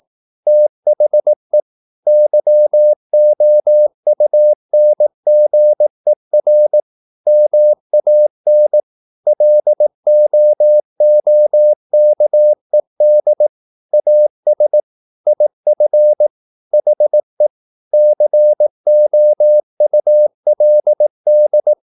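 Morse code, heard as a keyed tone, about 18 wpm, 600 Hz. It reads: THE YOUNGER MAN LOOKED AS IF HE COULD
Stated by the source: Morse practice at 18 wpm, English text, from CW med Gnister.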